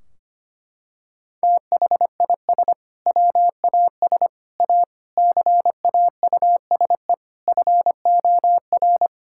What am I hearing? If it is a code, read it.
Morse code, heard as a keyed tone, about 25 wpm, 700 Hz. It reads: THIS WAS A CAUSE FOR
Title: Sentences from Top 1600 Words 25wpm